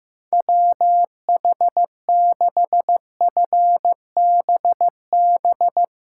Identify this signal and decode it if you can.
Morse code, heard as a keyed tone, about 15 wpm, 700 Hz. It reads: WH6FBB
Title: USA Callsign DE Callsign 15wpm